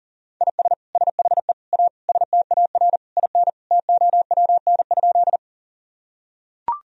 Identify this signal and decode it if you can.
Morse code, heard as a keyed tone, about 37 wpm, 700 Hz. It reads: IS SHE A STAR IN TOWN?
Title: Sentences from Top 1000 Words - Review Part 2 - 40wpm